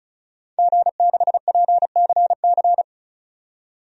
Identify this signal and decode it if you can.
Morse code, heard as a keyed tone, about 36 wpm, 700 Hz. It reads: G6PCC
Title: DXCC-100 Call Signs - 5 Characters - 35wpm